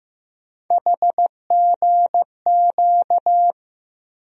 Morse code, heard as a keyed tone, about 15 wpm, 700 Hz. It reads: HGQ
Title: USA Call Sign Postfixes - 3 Letters 15wpm